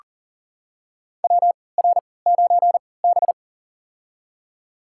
Morse code, heard as a keyed tone, about 40 wpm, spaced as for 23 wpm, 700 Hz. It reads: WR9B
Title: USA Call Signs - 4 Characters - 3x Character Spacing - 40wpm